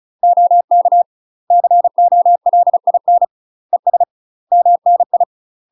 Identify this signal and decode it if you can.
Morse code, heard as a keyed tone, about 34 wpm, 700 Hz. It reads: OK COLIN ES MNI
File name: UK MASTER QSOs TRAINING / 35WPM / UK MASTER QSOs 0024